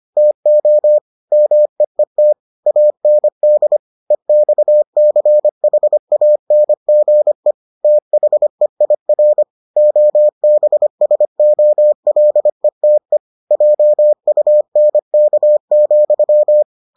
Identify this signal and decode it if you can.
Morse code, heard as a keyed tone, about 25 wpm, 600 Hz. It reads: TO MEET AND EXCHANGE THEIR OBSOLETE JUNK,